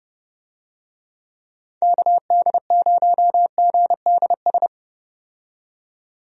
Morse code, heard as a keyed tone, about 32 wpm, 700 Hz. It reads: KD0GDS
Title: USA Call Signs - 6 Characters - 30wpm